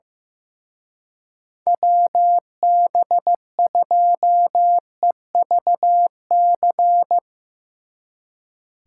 Morse code, heard as a keyed tone, about 16 wpm, 700 Hz. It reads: WB2EVC
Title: Speed Racing - USA 2x3 Call Signs 15wpm